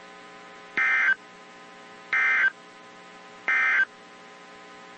Emergency Alert System (SAME) tones, heard as harsh buzzing bursts. It is heard from 0.7 to 3.9 s.
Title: WZ2523 - NOAA Weather Radio